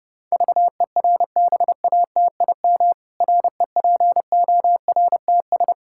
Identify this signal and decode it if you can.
Morse code, heard as a keyed tone, about 30 wpm, 700 Hz. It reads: VERBATIM REPORTS
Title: Pangram Challenge v2 - 30wpm